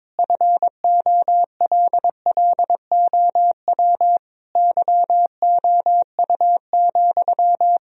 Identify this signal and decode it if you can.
Morse code, heard as a keyed tone, about 22 wpm, 700 Hz. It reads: FOLLOW YOU,
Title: Sentences from Top 1000 Words - Review Part 4 - 22wpm